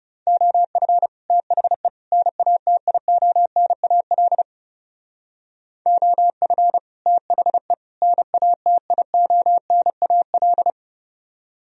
Morse code, heard as a keyed tone, about 37 wpm, 700 Hz. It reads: OF THE NATIONAL OF THE NATIONAL
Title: Speed-Racing - Sets of 3 Words 30wpm